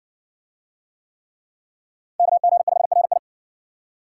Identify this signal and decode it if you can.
Morse code, heard as a keyed tone, about 60 wpm, 700 Hz. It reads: KG5RI